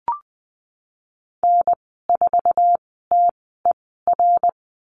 Morse code, heard as a keyed tone, about 20 wpm, spaced as for 15 wpm, 700 Hz. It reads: N4TER
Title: 055 Lesson 10g - Letter r - Call Signs 20wpm